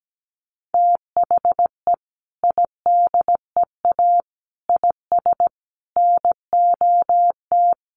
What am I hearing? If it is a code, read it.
Morse code, heard as a keyed tone, about 17 wpm, 700 Hz. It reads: THE IDEA IS NOT
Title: Sentences from Top 1300 Words 17wpm